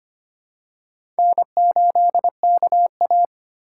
Morse code, heard as a keyed tone, about 25 wpm, 700 Hz. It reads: N8KA
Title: USA Call Signs - 4x Character Spacing - 25wpm